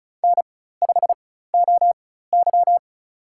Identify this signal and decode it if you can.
Morse code, heard as a keyed tone, about 34 wpm, spaced as for 16 wpm, 700 Hz. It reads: N5OY
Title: USA Call Signs - 4x Character Spacing - 35wpm